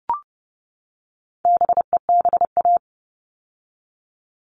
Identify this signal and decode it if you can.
Morse code, heard as a keyed tone, about 30 wpm, 700 Hz. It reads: BEBA
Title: All Cut Numbers - 4-Digit 30wpm